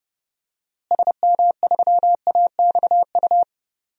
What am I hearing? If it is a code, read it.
Morse code, heard as a keyed tone, about 30 wpm, 700 Hz. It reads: SM3AXU
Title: DXCC-100 Call Signs - 6 Characters - Repeated 3x - 30wpm